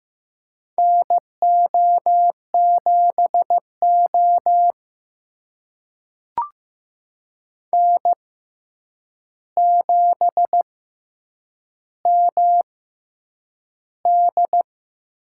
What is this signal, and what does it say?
Morse code, heard as a keyed tone, about 15 wpm, spaced as for 8 wpm, 700 Hz. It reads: NO7O N7MD
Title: USA Call Signs - 4 Characters - 6x Character Spacing - 15wpm